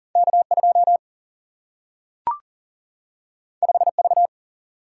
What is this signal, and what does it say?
Morse code, heard as a keyed tone, about 39 wpm, 700 Hz. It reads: K2 5V
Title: Single-Letter Single-Number 40wpm